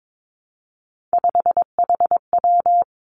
Morse code, heard as a keyed tone, about 22 wpm, 700 Hz. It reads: 5HW
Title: Three Challenging Letter-Number Combinations 22wpm